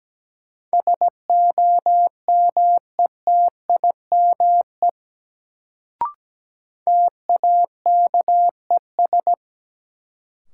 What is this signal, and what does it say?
Morse code, heard as a keyed tone, about 17 wpm, 700 Hz. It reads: SOMETIME TAKES
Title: Sentences from Top 1000 Words - Review Part 6 - New Word Drill - 17wpm